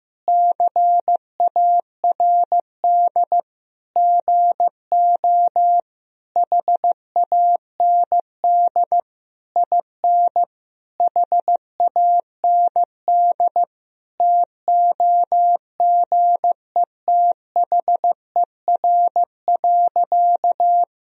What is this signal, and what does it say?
Morse code, heard as a keyed tone, about 15 wpm, 700 Hz. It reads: CARD GO HAND IN HAND TOGETHER.
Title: Sentences from Top 1000 Words - Review Part 4 - 15wpm